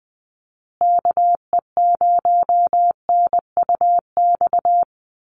Morse code, heard as a keyed tone, about 20 wpm, 700 Hz. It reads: KE0NUX
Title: USA Call Signs Repeated 3x 20wpm